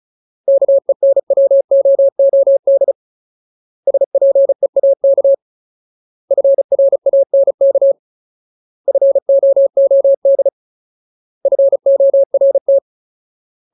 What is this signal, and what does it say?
Morse code, heard as a keyed tone, about 34 wpm, 550 Hz. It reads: KENWOOD SPEAK FRANK FOOD FORT
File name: Ultimate Word List / 35WPM Ultimate